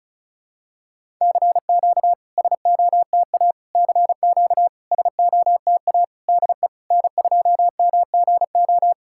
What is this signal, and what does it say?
Morse code, heard as a keyed tone, about 36 wpm, 700 Hz. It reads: CQ SOTA CQ SOTA DE N2MGO
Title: SOTA QSO Practice 35wpm